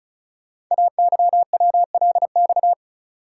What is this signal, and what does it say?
Morse code, heard as a keyed tone, about 36 wpm, 700 Hz. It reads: AYWLX
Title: Code Groups - 5 Letters - 35wpm